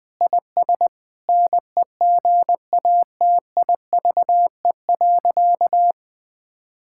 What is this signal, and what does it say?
Morse code, heard as a keyed tone, about 21 wpm, 700 Hz. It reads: IS NEGATIVE.
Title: Sentences from Top 1000 Words - Review Part 6 - 20wpm